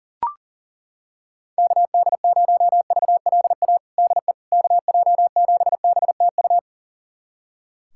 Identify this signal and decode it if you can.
Morse code, heard as a keyed tone, about 40 wpm, 700 Hz. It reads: KD0VLA DE KJ7BTU
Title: USA Callsign DE Callsign - 6 Characters - 40wpm